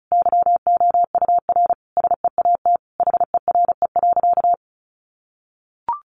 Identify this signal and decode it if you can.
Morse code, heard as a keyed tone, about 34 wpm, 700 Hz. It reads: YOUR SEAT HERE.
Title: Sentences from Top 1500 Words 35wpm